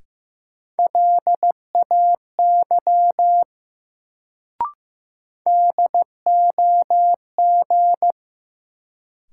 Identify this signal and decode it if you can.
Morse code, heard as a keyed tone, about 15 wpm, 700 Hz. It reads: LAY DOG
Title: Sentences from Top 400 Words - New Word Drill - 15wpm